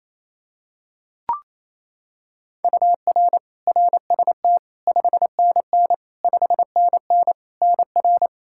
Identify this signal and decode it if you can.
Morse code, heard as a keyed tone, about 28 wpm, 700 Hz. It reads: UR RST 5NN 5NN NR
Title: Practice QSO - Exchange 1 - Beginning - 28wpm